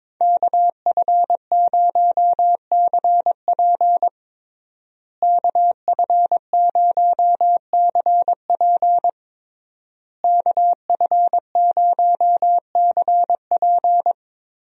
Morse code, heard as a keyed tone, about 22 wpm, 700 Hz. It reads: KF0CP KF0CP KF0CP
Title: USA Call Signs - 5 Characters - Repeated 3x - 22wpm